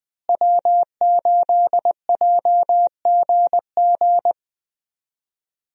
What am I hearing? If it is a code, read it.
Morse code, heard as a keyed tone, about 21 wpm, 700 Hz. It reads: W8JGG
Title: USA Call Signs - 5 Characters - Repeated 3x - 20wpm